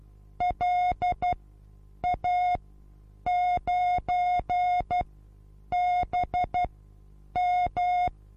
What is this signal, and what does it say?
Morse code, heard as a keyed tone, about 12 wpm, spaced as for 8 wpm, 720 Hz. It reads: LA9BM
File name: Onlinelehrgang / morsen / DJ4UF Morsen23